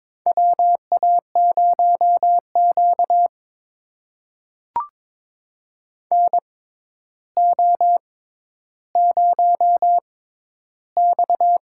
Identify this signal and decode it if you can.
Morse code, heard as a keyed tone, about 22 wpm, spaced as for 11 wpm, 700 Hz. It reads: WA0Q NO0X
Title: USA Call Signs - 4 Characters - 6x Character Spacing - 22wpm